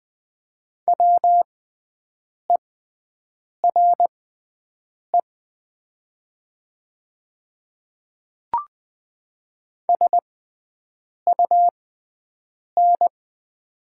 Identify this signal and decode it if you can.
Morse code, heard as a keyed tone, about 20 wpm, spaced as for 7 wpm, 700 Hz. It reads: WERE SUN
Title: 6x Character Spacing - Top 300 Words - 20wpm